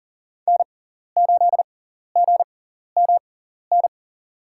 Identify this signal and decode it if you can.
Morse code, heard as a keyed tone, about 37 wpm, spaced as for 14 wpm, 700 Hz. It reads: N8GMN
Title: USA Call Signs - 6x Character Spacing - 40wpm